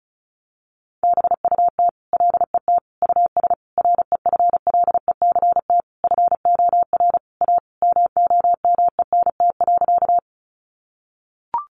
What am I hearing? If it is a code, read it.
Morse code, heard as a keyed tone, about 34 wpm, 700 Hz. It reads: BUT LET US REFLECT FOR A MOMENT.
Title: Sentences from Top 1500 Words 35wpm